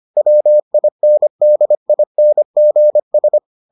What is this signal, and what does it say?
Morse code, heard as a keyed tone, about 25 wpm, 600 Hz. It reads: WINDINGS